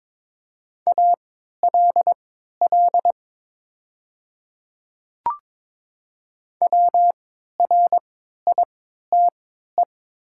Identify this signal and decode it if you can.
Morse code, heard as a keyed tone, about 22 wpm, spaced as for 13 wpm, 700 Hz. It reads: ALL WRITE
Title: 3x Character Spacing - Top 200 Words - 22wpm